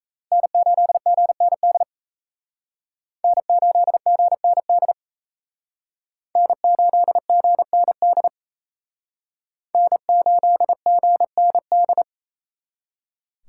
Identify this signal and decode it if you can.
Morse code, heard as a keyed tone, about 35 wpm, 700 Hz. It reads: N8GND N8GND N8GND N8GND